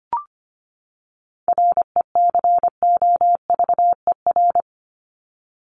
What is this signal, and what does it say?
Morse code, heard as a keyed tone, about 25 wpm, 700 Hz. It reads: RECOVER
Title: Top 1751-2000 Words 25wpm